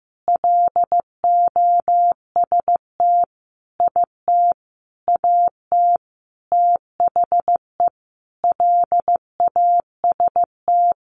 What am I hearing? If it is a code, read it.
Morse code, heard as a keyed tone, about 15 wpm, 700 Hz. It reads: LOST IT AT THE LAST
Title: Sentences from Top 1400 Words 15wpm